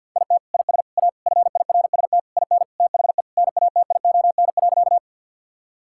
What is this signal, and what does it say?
Morse code, heard as a keyed tone, about 51 wpm, 700 Hz. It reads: IT IS A FIRST IN THE NATION.
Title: Sentences from Top 1000 Words - Review Part 6 - 50wpm